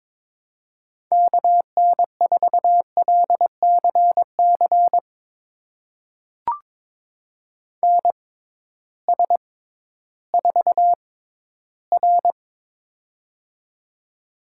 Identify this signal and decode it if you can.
Morse code, heard as a keyed tone, about 22 wpm, spaced as for 13 wpm, 700 Hz. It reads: KN4LCC NS4R
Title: USA Call Signs - 6x Character Spacing - 22wpm